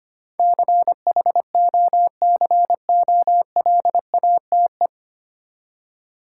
Morse code, heard as a keyed tone, about 25 wpm, 700 Hz. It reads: CHOCOLATE
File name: practice / 25wpm / Easy Words with Hints - 25wpm